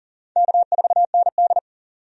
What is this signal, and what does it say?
Morse code, heard as a keyed tone, about 40 wpm, 700 Hz. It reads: K4ND
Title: USA Call Signs - 4 Characters - 40wpm